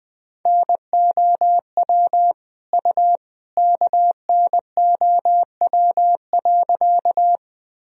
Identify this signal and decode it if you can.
Morse code, heard as a keyed tone, about 20 wpm, 700 Hz. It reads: NOW U KNOW.
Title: Sentences from Top 100 Words and Abbreviations 20wpm